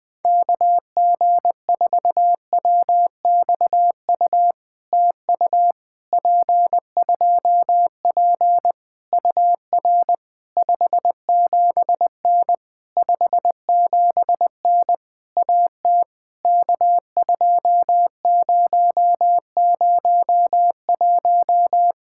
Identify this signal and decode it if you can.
Morse code, heard as a keyed tone, about 20 wpm, 700 Hz. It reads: KG4WXU TU P2P UR 57N 57N AT K2001
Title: POTA QSO Practice - 20wpm